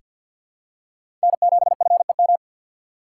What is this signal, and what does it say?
Morse code, heard as a keyed tone, about 47 wpm, 700 Hz. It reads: N7FEM
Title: USA Call Signs - 5 Characters - 4x Character Spacing - 50wpm